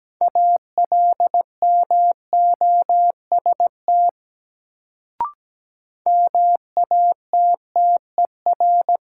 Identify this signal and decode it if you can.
Morse code, heard as a keyed tone, about 17 wpm, 700 Hz. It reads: ALMOST MATTER
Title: Sentences from Top 700 Words - New Word Drill - 17wpm